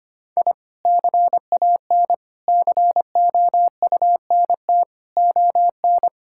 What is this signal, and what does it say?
Morse code, heard as a keyed tone, about 25 wpm, 700 Hz. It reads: I CAN COUNT ON
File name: Sets of 9 Words - 25wpm